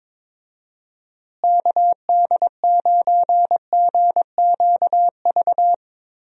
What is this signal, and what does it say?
Morse code, heard as a keyed tone, about 22 wpm, 700 Hz. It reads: KD9GQV
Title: USA Call Signs - 6 Characters - Repeated 3x - 22wpm